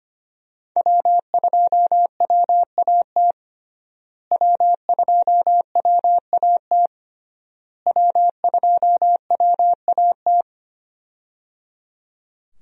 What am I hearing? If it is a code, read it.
Morse code, heard as a keyed tone, about 25 wpm, 700 Hz. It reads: W2WAT W2WAT W2WAT
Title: USA Call Signs - 5 Characters - Repeated 3x - 25wpm